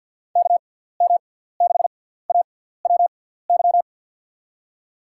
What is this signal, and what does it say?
Morse code, heard as a keyed tone, about 48 wpm, spaced as for 17 wpm, 700 Hz. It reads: KM6AWY
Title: USA Call Signs - 6 Characters - 6x Character Spacing - 50wpm